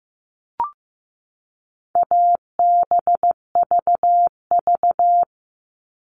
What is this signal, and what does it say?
Morse code, heard as a keyed tone, about 15 wpm, 700 Hz. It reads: ABVV
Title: All Cut Numbers - 4-Digit 15wpm